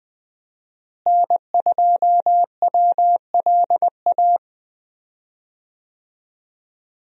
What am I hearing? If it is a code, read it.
Morse code, heard as a keyed tone, about 20 wpm, 700 Hz. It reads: N2WLA
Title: USA Call Signs - 5 Characters - Repeated 2x - 20wpm